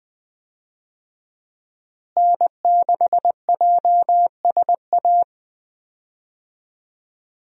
Morse code, heard as a keyed tone, about 20 wpm, 700 Hz. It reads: N6JSA
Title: USA Call Signs - 5 Characters - Repeated 3x - 20wpm